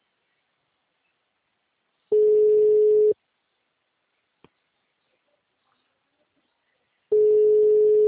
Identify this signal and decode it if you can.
Ringback tone (Europe, 425 Hz).
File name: out- -2013- - - .33